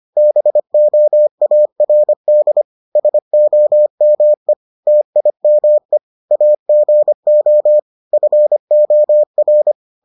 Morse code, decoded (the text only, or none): BOARD SOME TIME AGO FOR